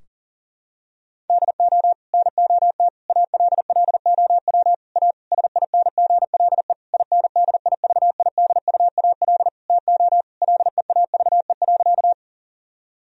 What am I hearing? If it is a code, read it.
Morse code, heard as a keyed tone, about 40 wpm, 700 Hz. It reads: DO NOT ALLOW A SINGLE INDIVIDUAL TO LEAVE.
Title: Sentences from Top 1000 Words - Review Part 4 - 40wpm